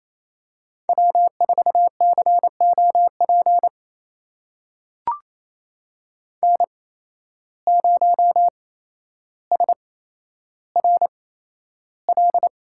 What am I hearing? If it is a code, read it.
Morse code, heard as a keyed tone, about 27 wpm, spaced as for 12 wpm, 700 Hz. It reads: W4COP N0SRL
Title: USA Call Signs - 5 Characters - 8x Character Spacing - 28wpm